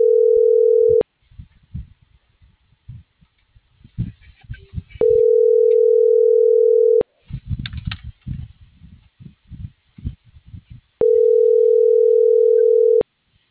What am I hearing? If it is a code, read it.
Ringback tone (North America), 440 + 480 Hz.